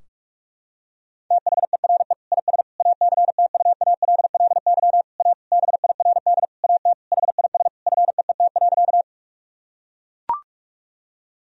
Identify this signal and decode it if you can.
Morse code, heard as a keyed tone, about 41 wpm, 700 Hz. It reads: THERE IS ACTUALLY A BIRD AT HIS FEET.
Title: Sentences from Top 1000 Words - Review Part 4 - 45wpm